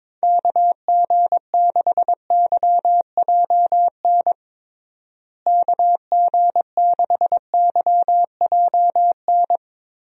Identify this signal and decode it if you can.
Morse code, heard as a keyed tone, about 21 wpm, 700 Hz. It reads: KG6YJN KG6YJN
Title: USA Call Signs - 6 Characters - Repeated 2x - 22wpm